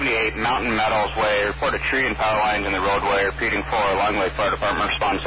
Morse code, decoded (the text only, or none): none